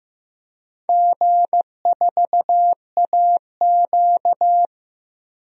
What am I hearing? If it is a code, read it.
Morse code, heard as a keyed tone, about 15 wpm, 700 Hz. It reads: G4AQ
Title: DXCC-100 Call Signs - 4 Characters - Repeated 2x - 15wpm